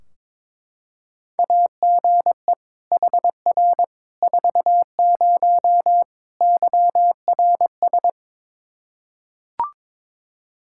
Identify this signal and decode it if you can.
Morse code, heard as a keyed tone, about 22 wpm, 700 Hz. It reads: AGE HR 40 YRS
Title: Practice QSO - Exchange 3 - More Info - 22wpm